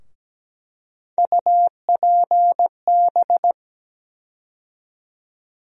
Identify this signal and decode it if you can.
Morse code, heard as a keyed tone, about 17 wpm, 700 Hz. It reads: UPB
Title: USA Call Sign Postfixes - 3 Letters 17wpm